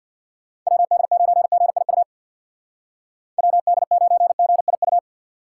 Morse code, heard as a keyed tone, about 63 wpm, 700 Hz. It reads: WD9GIU WD9GIU